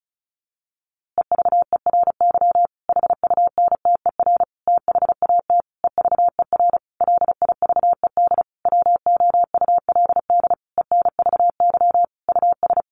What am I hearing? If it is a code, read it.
Morse code, heard as a keyed tone, about 34 wpm, 700 Hz. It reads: EVERY HUNTER THAT EVER LIVED WOULD ENVY US
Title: A Sound Of Thunder 35wpm